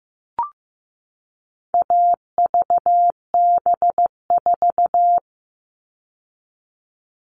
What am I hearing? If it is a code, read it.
Morse code, heard as a keyed tone, about 15 wpm, 700 Hz. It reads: AVB4
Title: All Cut Numbers - 4-Digit 15wpm